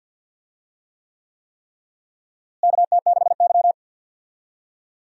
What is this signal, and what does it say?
Morse code, heard as a keyed tone, about 51 wpm, 700 Hz. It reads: KT6Y
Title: USA Call Signs - 4 Characters - Repeated 3x - 50wpm